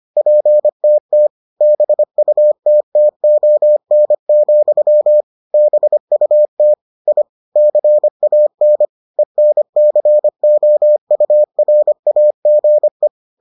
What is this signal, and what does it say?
Morse code, heard as a keyed tone, about 25 wpm, 600 Hz. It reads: PTT BUTTON, BUT I CAN ENCOURAGE